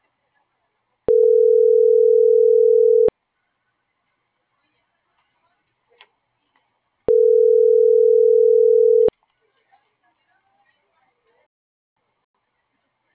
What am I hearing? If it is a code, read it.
Ringback tone (North America), 440 + 480 Hz.